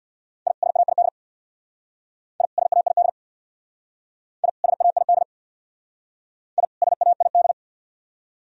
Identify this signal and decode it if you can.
Morse code, heard as a keyed tone, about 63 wpm, 700 Hz. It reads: I SAID I SAID I SAID I SAID